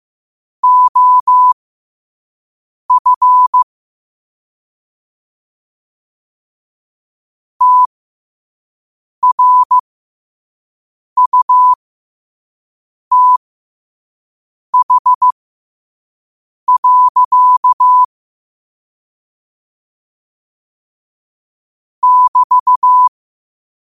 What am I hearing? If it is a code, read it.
Morse code, heard as a keyed tone, about 15 wpm, spaced as for 5 wpm, 1000 Hz. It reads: OF TRUTH. <BT>
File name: qotdfiles / 5wpm